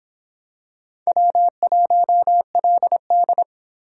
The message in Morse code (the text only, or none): W1LD